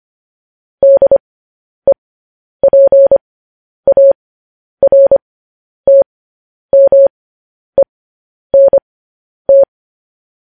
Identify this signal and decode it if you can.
Morse code, heard as a keyed tone, about 25 wpm, spaced as for 10 wpm, 550 Hz. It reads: DEPARTMENT
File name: podcast / News headlines-1x-25x10